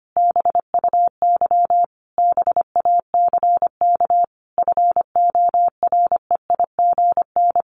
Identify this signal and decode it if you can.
Morse code, heard as a keyed tone, about 25 wpm, 700 Hz. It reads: BUY BACK FOREIGN
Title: Sentences from Top 1500 Words 25wpm